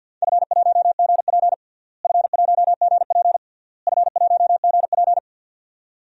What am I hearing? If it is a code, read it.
Morse code, heard as a keyed tone, about 47 wpm, 700 Hz. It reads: F1GP F1GP F1GP
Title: DXCC-100 Call Signs - 4 Characters - Repeated 3x - 50wpm